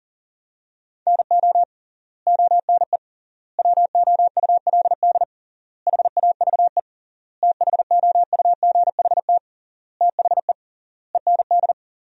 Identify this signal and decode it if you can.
Morse code, heard as a keyed tone, about 40 wpm, 700 Hz. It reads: NO ONE WOULD HAVE THOUGHT THE END
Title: 3x Word-Spacing - Sentences from Top 300 Words - 40wpm